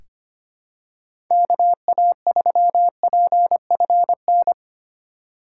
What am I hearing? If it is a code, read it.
Morse code, heard as a keyed tone, about 25 wpm, 700 Hz. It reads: KA3PFN
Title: USA Call Signs - 6 Characters - 2x Character Spacing - 25wpm